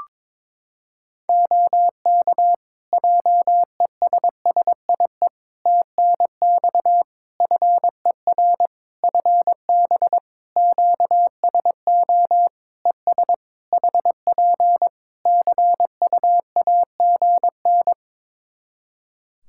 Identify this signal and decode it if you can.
Morse code, heard as a keyed tone, about 22 wpm, 700 Hz. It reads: OK JESSIE TNX FER FB QSO ES HP CUAGN
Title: Practice QSO - Exchange 4 - Ending - 22wpm